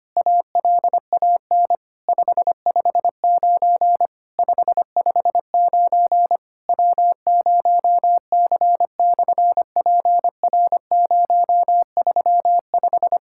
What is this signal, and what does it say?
Morse code, heard as a keyed tone, about 25 wpm, 700 Hz. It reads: ALAN 559 559 W0C/PR035
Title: SOTA QSO Practice 25wpm